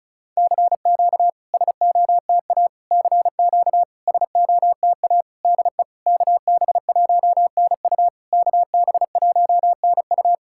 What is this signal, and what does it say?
Morse code, heard as a keyed tone, about 36 wpm, 690 Hz. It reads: CQ SOTA CQ SOTA DE KB1NU KB1NU
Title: SOTA QSO Practice 35wpm